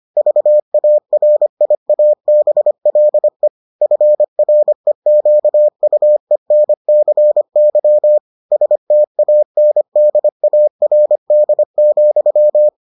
Morse code, heard as a keyed tone, about 25 wpm, 600 Hz. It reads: VARIABLE FREQUENCY STANDARD,